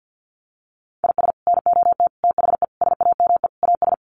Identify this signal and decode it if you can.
Morse code, heard as a keyed tone, about 50 wpm, 700 Hz. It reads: IS NOT THE SAME AS
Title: Sets of 5 Words Encore 50wpm